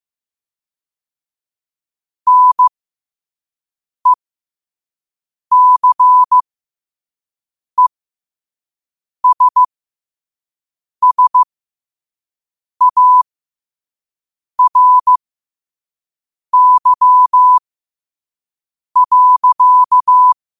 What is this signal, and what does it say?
Morse code, heard as a keyed tone, about 15 wpm, spaced as for 5 wpm, 1000 Hz. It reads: NECESSARY.
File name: qotdfiles / 5wpm